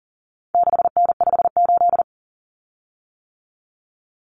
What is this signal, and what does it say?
Morse code, heard as a keyed tone, about 38 wpm, 700 Hz. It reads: 6N58
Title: Common Cut Numbers - 4-Digit 40wpm